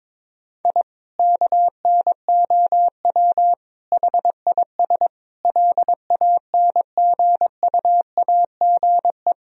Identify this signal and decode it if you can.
Morse code, heard as a keyed tone, about 22 wpm, 700 Hz. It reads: I KNOW HIS LANGUAGE
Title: Sentences from Top 1000 Words - Review Part 3 - 22wpm